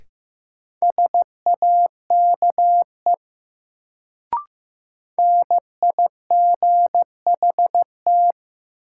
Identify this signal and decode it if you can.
Morse code, heard as a keyed tone, about 15 wpm, 700 Hz. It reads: SAKE NIGHT
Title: Sets of 4 Words - New Word Drill - 15wpm